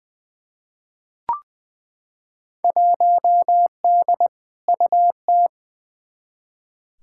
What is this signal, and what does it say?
Morse code, heard as a keyed tone, about 20 wpm, 700 Hz. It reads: 1D UT
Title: Field Day Exchanges - 20wpm